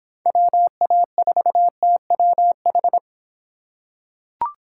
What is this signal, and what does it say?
Morse code, heard as a keyed tone, about 24 wpm, 700 Hz. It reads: WA4TWH